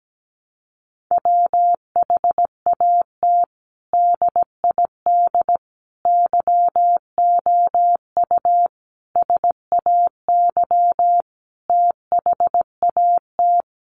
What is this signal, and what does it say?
Morse code, heard as a keyed tone, about 17 wpm, 700 Hz. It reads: WHAT DID YOU SAY THAT
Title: Sentences from Top 200 Words 17wpm